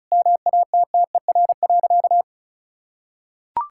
Morse code, heard as a keyed tone, about 36 wpm, 700 Hz. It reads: MATTER.
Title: Sentences from Top 1600 Words 35wpm